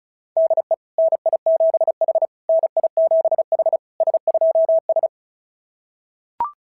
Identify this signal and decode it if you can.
Morse code, heard as a keyed tone, about 36 wpm, 660 Hz. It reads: DE NI7H NI7H S2S
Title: SOTA QSO Practice 35wpm